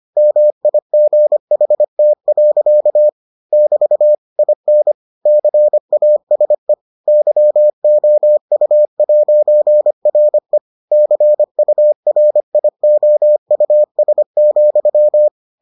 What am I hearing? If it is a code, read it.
Morse code, heard as a keyed tone, about 25 wpm, 600 Hz. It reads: MIGHT. <BT> IN CASE YOU'RE CURIOUS,